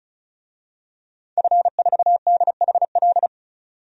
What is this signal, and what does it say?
Morse code, heard as a keyed tone, about 34 wpm, 700 Hz. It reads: F4DHL
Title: DXCC-100 Call Signs - 5 Characters - 35wpm